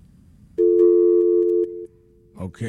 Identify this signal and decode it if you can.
Dial tone (North America), 350 + 440 Hz.